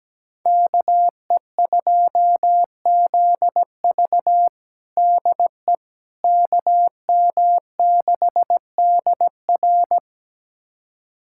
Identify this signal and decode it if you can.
Morse code, heard as a keyed tone, about 17 wpm, 700 Hz. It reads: KE2ZV DE KM6DR
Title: USA Callsign DE Callsign - 5 Characters - 17wpm